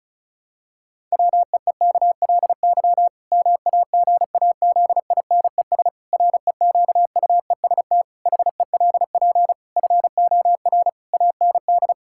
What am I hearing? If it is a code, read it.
Morse code, heard as a keyed tone, about 34 wpm, 700 Hz. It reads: WEEKLY MAGAZINES REQUEST HELP FOR AND
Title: Pangram Challenge v2 - 35wpm